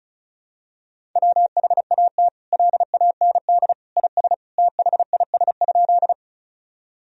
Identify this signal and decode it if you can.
Morse code, heard as a keyed tone, about 36 wpm, 700 Hz. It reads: WHAT LAND IS THIS?
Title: Sentences from Top 1000 Words - Review Part 1 - 35wpm